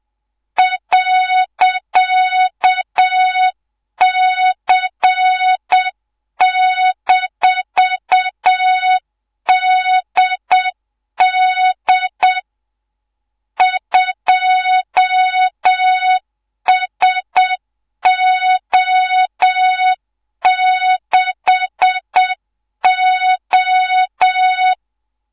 Morse code, heard as a keyed tone, about 7 wpm, 3700 Hz, with noise. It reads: .C-DD 2SO6O
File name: Morze-HA2MN / hangp / 2 03 GYAKORLAT